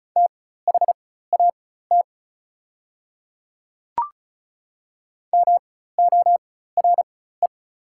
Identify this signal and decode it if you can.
Morse code, heard as a keyed tone, about 36 wpm, spaced as for 16 wpm, 690 Hz. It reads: THAT MORE
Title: 4x Character Spacing - Top 100 Words - 35wpm